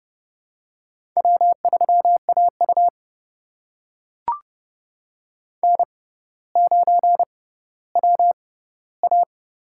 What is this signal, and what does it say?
Morse code, heard as a keyed tone, about 30 wpm, spaced as for 15 wpm, 690 Hz. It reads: W3AU N9WA
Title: USA Call Signs - 4 Characters - 6x Character Spacing - 30wpm